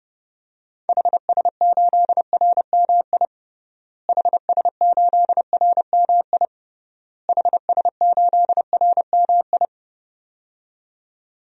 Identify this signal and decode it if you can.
Morse code, heard as a keyed tone, about 32 wpm, 700 Hz. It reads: HS8RMI HS8RMI HS8RMI
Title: DXCC-100 Call Signs - 6 Characters - Repeated 3x - 30wpm